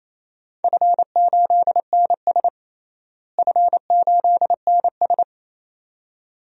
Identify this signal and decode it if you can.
Morse code, heard as a keyed tone, about 28 wpm, 700 Hz. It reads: F8NS F8NS
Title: DXCC-100 Call Signs - 4 Characters - Repeated 2x - 28wpm